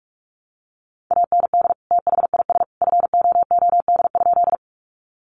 Morse code, heard as a keyed tone, about 44 wpm, 700 Hz. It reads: AND THIS FOOD?
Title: Sentences from Top 300 Words 45wpm